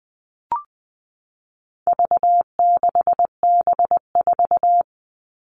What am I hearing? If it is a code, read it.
Morse code, heard as a keyed tone, about 20 wpm, 700 Hz. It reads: V6B4
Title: All Cut Numbers - 4-Digit 20wpm